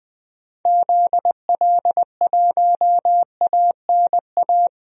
Morse code, heard as a keyed tone, about 20 wpm, 700 Hz. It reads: ZL1ANA
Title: DXCC-100 Call Signs - 6 Characters - Repeated 3x - 20wpm